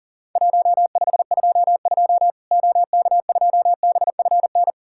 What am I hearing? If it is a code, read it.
Morse code, heard as a keyed tone, about 40 wpm, 700 Hz. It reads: 1522 OK2BFN
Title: Worked All Europe DX Contest - QTC Exchange - 40wpm